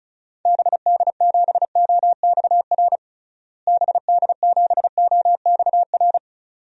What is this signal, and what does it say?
Morse code, heard as a keyed tone, about 36 wpm, 700 Hz. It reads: BD7OXR BD7OXR
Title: DXCC-100 Call Signs - 6 Characters - Repeated 2x - 35wpm